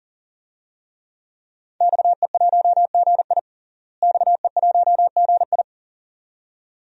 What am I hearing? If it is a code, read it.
Morse code, heard as a keyed tone, about 40 wpm, 700 Hz. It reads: XE1GI XE1GI
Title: DXCC-100 Call Signs - 5 Characters - Repeated 2x - 40wpm